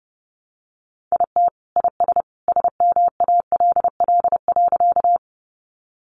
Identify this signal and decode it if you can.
Morse code, heard as a keyed tone, about 28 wpm, 700 Hz. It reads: IT IS SMALL.
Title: Easy Phrases 30wpm